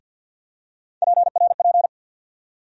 Morse code, heard as a keyed tone, about 51 wpm, 700 Hz. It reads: PRP
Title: USA Call Sign Postfixes - 3 Letters 50wpm